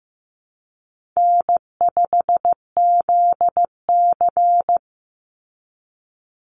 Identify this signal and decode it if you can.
Morse code, heard as a keyed tone, about 15 wpm, 700 Hz. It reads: N5ZC
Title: NA SOTA Chaser Callsigns 15wpm